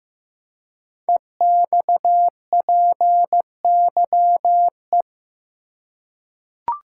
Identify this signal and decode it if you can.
Morse code, heard as a keyed tone, about 15 wpm, 700 Hz. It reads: EXPYE
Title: Code Groups - 5 Letters - 15wpm